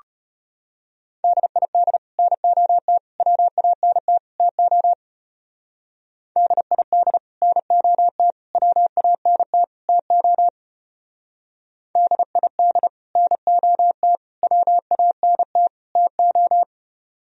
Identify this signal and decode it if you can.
Morse code, heard as a keyed tone, about 32 wpm, 700 Hz. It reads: DID NOT WANT TO DID NOT WANT TO DID NOT WANT TO